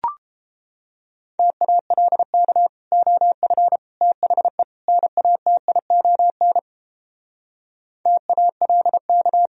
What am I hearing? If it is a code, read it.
Morse code, heard as a keyed tone, about 29 wpm, 700 Hz. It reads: TALK OF THE NATION TALK